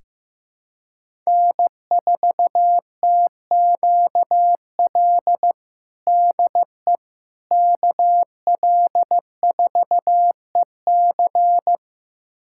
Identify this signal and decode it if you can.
Morse code, heard as a keyed tone, about 15 wpm, 700 Hz. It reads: N4TQL DE KL4EC
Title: USA Callsign DE Callsign - 5 Characters - 15wpm